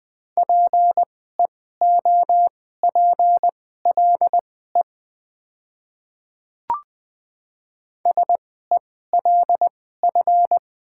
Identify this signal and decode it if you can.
Morse code, heard as a keyed tone, about 20 wpm, spaced as for 14 wpm, 700 Hz. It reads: PEOPLE SELF
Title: 2x Character Spacing - Top 200 Words - 20wpm